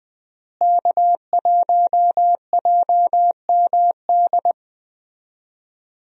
Morse code, heard as a keyed tone, about 20 wpm, 700 Hz. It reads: K1JMD
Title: USA Call Signs - 5 Characters - Repeated 2x - 20wpm